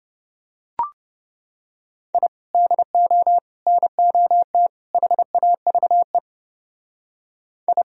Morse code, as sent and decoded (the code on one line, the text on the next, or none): .. / -.. --- / -. --- - / .... .- ...- . / ..
I DO NOT HAVE I